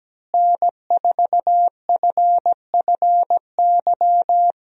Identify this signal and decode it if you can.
Morse code, heard as a keyed tone, about 17 wpm, 700 Hz. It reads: N4FFY
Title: USA Call Signs - 5 Characters - Repeated 3x - 17wpm